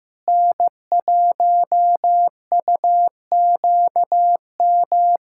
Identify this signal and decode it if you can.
Morse code, heard as a keyed tone, about 15 wpm, 700 Hz. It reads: N1UQM
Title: USA Callsign DE Callsign - 5 Characters - 15wpm